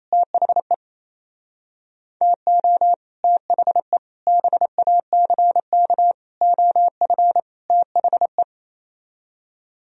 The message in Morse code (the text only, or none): THE TO THE BACK OF THE